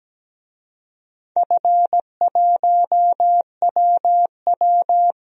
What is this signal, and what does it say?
Morse code, heard as a keyed tone, about 17 wpm, 700 Hz. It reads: F1WW
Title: DXCC-100 Call Signs - 4 Characters - Repeated 3x - 17wpm